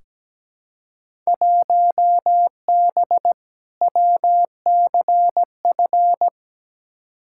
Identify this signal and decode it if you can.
Morse code, heard as a keyed tone, about 17 wpm, 700 Hz. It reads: 1B WCF
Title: Field Day Exchanges Encore - 17wpm